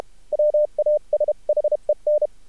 Morse code, heard as a keyed tone, about 32 wpm, 600 Hz. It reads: WASHEN
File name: archived / CRSnet